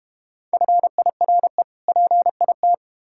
Morse code, heard as a keyed tone, about 32 wpm, 700 Hz. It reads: FIRE PIT